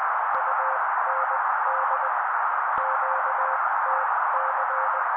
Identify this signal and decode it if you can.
Morse code, heard as a keyed tone, about 20 wpm, spaced as for 16 wpm, 530 Hz. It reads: UND QTC
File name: daily / 16wpm noise